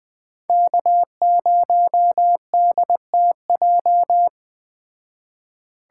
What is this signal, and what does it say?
Morse code, heard as a keyed tone, about 21 wpm, 700 Hz. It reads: K0DTJ